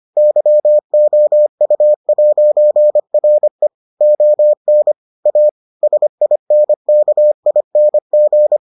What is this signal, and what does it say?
Morse code, heard as a keyed tone, about 25 wpm, 600 Hz. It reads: YOU'RE ON A SINKING